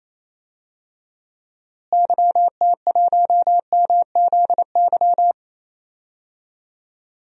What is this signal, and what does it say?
Morse code, heard as a keyed tone, about 28 wpm, 700 Hz. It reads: YT1MZY
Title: DXCC-100 Call Signs - 6 Characters - Repeated 2x - 28wpm